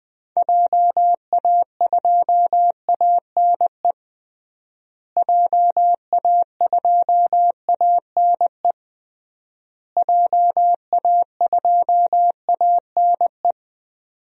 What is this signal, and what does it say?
Morse code, heard as a keyed tone, about 20 wpm, 700 Hz. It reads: JA2ANE JA2ANE JA2ANE
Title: DXCC-100 Call Signs - 6 Characters - Repeated 3x - 20wpm